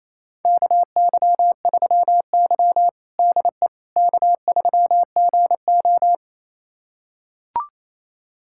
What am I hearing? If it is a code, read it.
Morse code, heard as a keyed tone, about 28 wpm, 700 Hz. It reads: KY3Y DE K3GO
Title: USA Callsign DE Callsign - 4 Characters - 28wpm